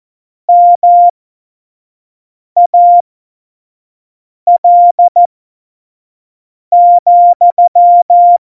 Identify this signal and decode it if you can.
Morse code, heard as a keyed tone, about 14 wpm, spaced as for 5 wpm, 700 Hz. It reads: MAL,
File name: Morsekurs / Witze / Witz 14 April 2020, 25 F70